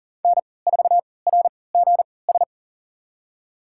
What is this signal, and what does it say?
Morse code, heard as a keyed tone, about 40 wpm, spaced as for 23 wpm, 700 Hz. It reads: N4RGS
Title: USA Call Signs - 3x Character Spacing - 40wpm